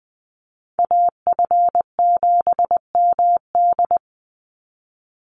Morse code, heard as a keyed tone, about 20 wpm, 700 Hz. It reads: AF7MD
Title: NA SOTA Chaser Callsigns 20wpm